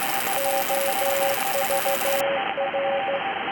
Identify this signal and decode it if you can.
Morse code, heard as a keyed tone, about 14 wpm, 550 Hz, with radio noise. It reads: OVR